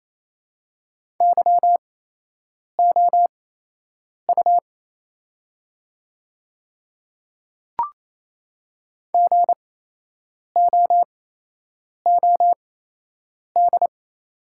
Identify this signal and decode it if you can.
Morse code, heard as a keyed tone, about 28 wpm, spaced as for 8 wpm, 700 Hz. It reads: YOU GOOD
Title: 8x Character Spacing - Top 300 Words - 28wpm